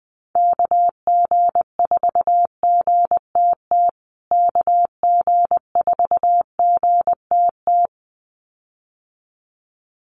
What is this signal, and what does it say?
Morse code, heard as a keyed tone, about 20 wpm, 700 Hz. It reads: KG4GTT KG4GTT
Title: USA Call Signs Repeated 2x 20wpm